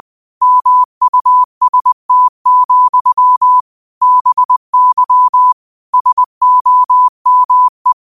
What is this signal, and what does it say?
Morse code, heard as a keyed tone, about 20 wpm, 1000 Hz. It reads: MUST, BY SOME
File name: qotdfiles / 20wpm